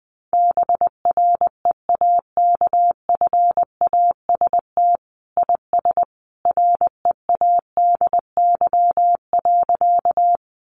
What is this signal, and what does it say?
Morse code, heard as a keyed tone, about 20 wpm, 700 Hz. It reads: BREAKFAST IS READY.
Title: Sentences from Top 1500 Words 20wpm